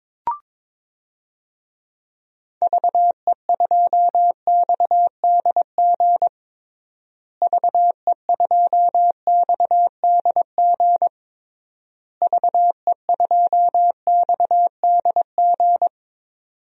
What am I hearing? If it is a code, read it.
Morse code, heard as a keyed tone, about 22 wpm, 700 Hz. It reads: VE2XDG VE2XDG VE2XDG
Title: DXCC-100 Call Signs - 6 Characters - Repeated 3x - 22wpm